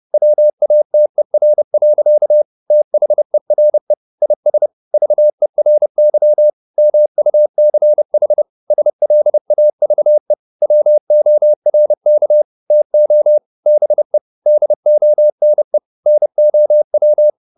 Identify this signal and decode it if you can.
Morse code, heard as a keyed tone, about 30 wpm, 600 Hz. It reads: WATER. THERE IS VERY MUCH SLAVE WORK TO BE DONE NOW